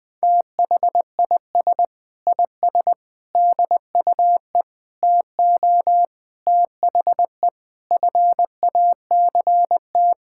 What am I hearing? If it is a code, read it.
Morse code, heard as a keyed tone, about 20 wpm, 700 Hz. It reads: THIS IS DUE TO THE FACT
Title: Sentences with 8-Word Phrases - Spoken Beginning - 20wpm